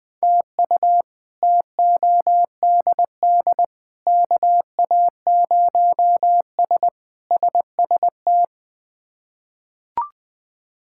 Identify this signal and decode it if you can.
Morse code, heard as a keyed tone, about 20 wpm, 700 Hz. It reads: TU TODD KA0S SST
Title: K1USN SST QSO Practice - 20wpm